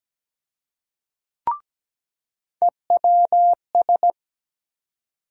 Morse code, heard as a keyed tone, about 17 wpm, 700 Hz. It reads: EWS
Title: USA Call Sign Postfixes - 3 Letters 17wpm